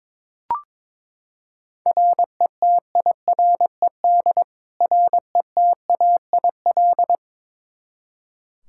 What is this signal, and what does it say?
Morse code, heard as a keyed tone, about 22 wpm, 700 Hz. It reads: RETIRED RETAIL
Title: K1USN SST QSO Practice 22wpm